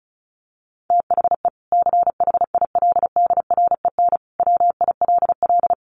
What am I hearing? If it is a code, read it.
Morse code, heard as a keyed tone, about 34 wpm, 700 Hz. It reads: THE CHILDREN WILL
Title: Sentences from Top 300 Words 35wpm